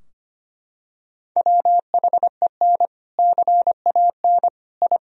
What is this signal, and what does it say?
Morse code, heard as a keyed tone, about 25 wpm, 700 Hz. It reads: WHEN CAN I
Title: Sentences from Top 1000 Words - Review Part 3 - 25wpm